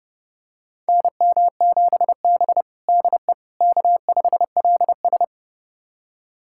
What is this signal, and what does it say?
Morse code, heard as a keyed tone, about 30 wpm, 700 Hz. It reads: NM7B DE K5LS
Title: USA Callsign DE Callsign - 4 Characters - 30wpm